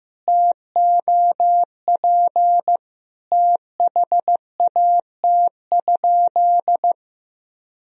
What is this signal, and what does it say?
Morse code, heard as a keyed tone, about 15 wpm, 700 Hz. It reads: TOP THAT?
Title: Sentences from Top 1000 Words - Review Part 2 - 15wpm